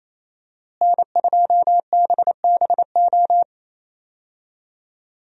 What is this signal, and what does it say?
Morse code, heard as a keyed tone, about 28 wpm, 700 Hz. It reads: N2BBO
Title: USA Call Signs - 5 Characters - Repeated 3x - 28wpm